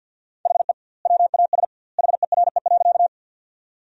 Morse code, heard as a keyed tone, about 50 wpm, 700 Hz. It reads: HE WAS HERE.
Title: 2x Word-Spacing - Sentences from Top 200 Words - 50wpm